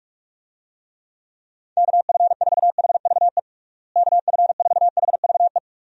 Morse code, heard as a keyed tone, about 46 wpm, 700 Hz. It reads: KF4HVE KF4HVE
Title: USA Call Signs - 6 Characters - Repeated 2x - 45wpm